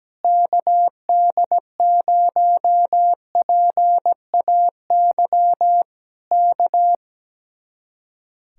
Morse code, heard as a keyed tone, about 17 wpm, 700 Hz. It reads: KD0PAY K
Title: Speed-Racing Common QSO Phrases Encore - 17wpm